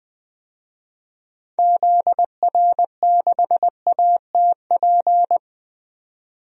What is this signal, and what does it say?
Morse code, heard as a keyed tone, about 20 wpm, 700 Hz. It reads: ZR6ATP
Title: DXCC-100 Call Signs - 6 Characters - 20wpm